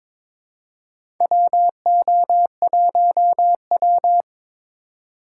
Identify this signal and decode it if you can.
Morse code, heard as a keyed tone, about 22 wpm, 700 Hz. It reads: WO1W
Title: USA Call Signs - 4 Characters - 4x Character Spacing - 22wpm